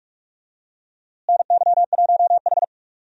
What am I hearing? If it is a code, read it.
Morse code, heard as a keyed tone, about 42 wpm, 700 Hz. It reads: NY1H
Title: USA Call Signs - 4 Characters - Repeated 2x - 45wpm